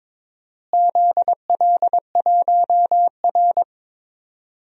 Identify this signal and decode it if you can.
Morse code, heard as a keyed tone, about 21 wpm, 700 Hz. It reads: ZL1R
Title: DXCC-100 Call Signs - 4 Characters - Repeated 3x - 22wpm